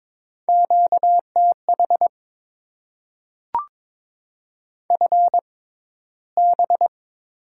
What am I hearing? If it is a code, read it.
Morse code, heard as a keyed tone, about 22 wpm, spaced as for 14 wpm, 690 Hz. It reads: QTH FB
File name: Top 20 QSO Elements - 6x Character Spacing - 22wpm